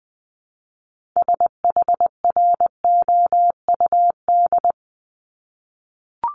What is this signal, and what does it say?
Morse code, heard as a keyed tone, about 20 wpm, 700 Hz. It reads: SHROUD
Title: Multi-letter Phonemes 20wpm